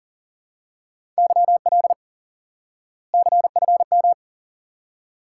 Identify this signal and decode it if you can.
Morse code, heard as a keyed tone, about 40 wpm, 700 Hz. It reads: YL CFM
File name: IQE Course 3 - Lesson 89 - YAGI - Two Elements - 40wpm